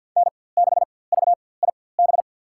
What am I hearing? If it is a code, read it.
Morse code, heard as a keyed tone, about 50 wpm, spaced as for 23 wpm, 700 Hz. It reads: N6VIB
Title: USA Call Signs - 4x Character Spacing - 50wpm